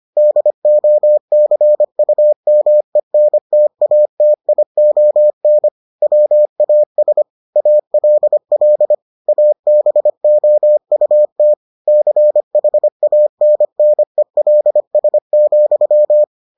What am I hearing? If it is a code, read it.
Morse code, heard as a keyed tone, about 25 wpm, 600 Hz. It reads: DOCUMENTATION WAS ALL ABOUT CHANNELS,